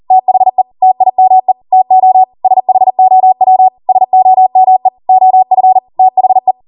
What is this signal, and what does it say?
Morse code, heard as a keyed tone, about 40 wpm, 750 Hz. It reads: THE TIME TO SHOW SOME OF THE